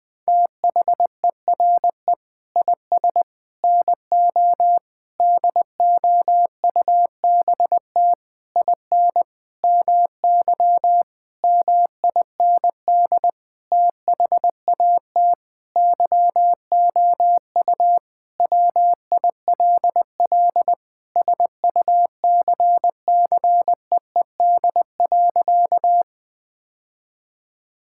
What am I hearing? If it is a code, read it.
Morse code, heard as a keyed tone, about 20 wpm, 700 Hz. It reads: THERE IS NO DOUBT IN MY MIND THAT YOU WILL SUCCEED.
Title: Sentences with 8-Word Phrases - 20wpm